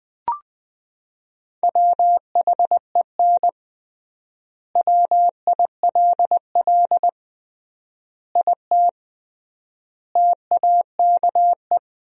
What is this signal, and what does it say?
Morse code, heard as a keyed tone, about 21 wpm, 700 Hz. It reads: WHEN WILL IT TAKE
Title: 3x Word-Spacing - Sentences from Top 200 Words - 20wpm